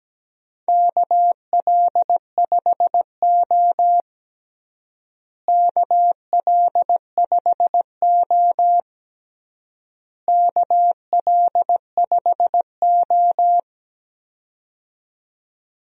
Morse code, heard as a keyed tone, about 17 wpm, 700 Hz. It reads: KL5O KL5O KL5O
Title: USA Call Signs - 4 Characters - Repeated 3x - 17wpm